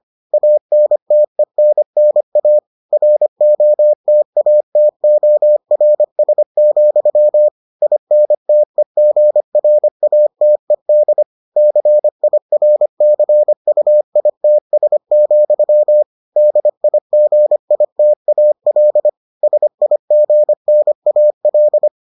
Morse code, heard as a keyed tone, about 25 wpm, 600 Hz. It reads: ANTENNA ROTATORS, INTEGRATED CIRCUITS, DIGITAL SIGNAL